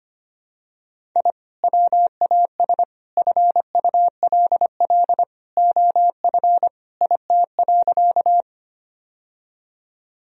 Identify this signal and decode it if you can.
Morse code, heard as a keyed tone, about 25 wpm, 700 Hz. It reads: I WAS FULL OF IT.
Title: Sentences from Top 1000 Words - Review Part 3 - 25wpm